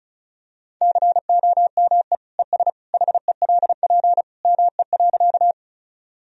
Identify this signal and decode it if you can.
Morse code, heard as a keyed tone, about 34 wpm, 690 Hz. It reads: COME ES HELP ME.
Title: Sentences from Top 200 Words and Abbreviations 35wpm